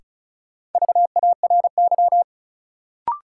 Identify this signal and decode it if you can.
Morse code, heard as a keyed tone, about 36 wpm, 700 Hz. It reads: VARY
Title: Sentences from Top 1200 Words - New Word Drill Alt - 35wpm